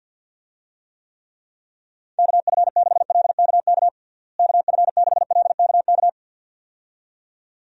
Morse code, heard as a keyed tone, about 51 wpm, 700 Hz. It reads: KF6LKK KF6LKK
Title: USA Call Signs - 6 Characters - Repeated 2x - 50wpm